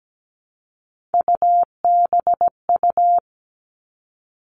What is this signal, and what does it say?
Morse code, heard as a keyed tone, about 17 wpm, 700 Hz. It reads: UBU
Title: Three Challenging Letter-Number Combinations 17wpm